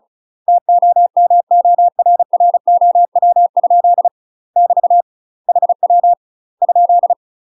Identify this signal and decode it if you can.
Morse code, heard as a keyed tone, about 36 wpm, 700 Hz. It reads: TOMORROW? <BT> HW ?
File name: UK MASTER QSOs TRAINING / 35WPM / UK MASTER QSOs 0035